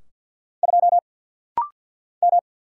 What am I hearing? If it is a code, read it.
Morse code, heard as a keyed tone, about 48 wpm, 700 Hz. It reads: 2 M
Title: Single Letter-Number - Rapid-Fire 50wpm